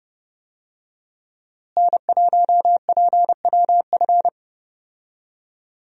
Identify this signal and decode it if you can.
Morse code, heard as a keyed tone, about 29 wpm, 700 Hz. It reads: N1PWF
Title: USA Call Signs - 5 Characters - Repeated 3x - 30wpm